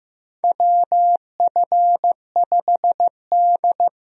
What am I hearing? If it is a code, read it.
Morse code, heard as a keyed tone, about 15 wpm, 700 Hz. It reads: WF5D
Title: USA Call Signs - 4 Characters - 15wpm